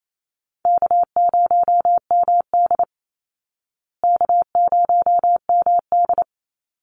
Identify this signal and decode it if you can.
Morse code, heard as a keyed tone, about 28 wpm, 700 Hz. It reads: K0MD K0MD
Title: Call Signs Repeated 2x - 28wpm